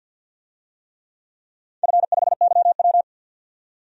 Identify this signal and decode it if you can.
Morse code, heard as a keyed tone, about 48 wpm, 700 Hz. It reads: F5YW
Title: DXCC-100 Call Signs - 4 Characters - 50wpm